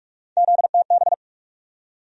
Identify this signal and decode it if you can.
Morse code, heard as a keyed tone, about 48 wpm, 700 Hz. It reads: ZTB
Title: USA Call Sign Postfixes - 3 Letters 45wpm